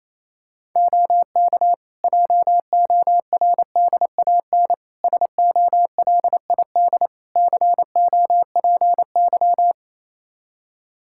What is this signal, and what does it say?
Morse code, heard as a keyed tone, about 27 wpm, 700 Hz. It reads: OK JORDAN SOLID COPY
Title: Practice QSO - Exchange 3 - More Info - 28wpm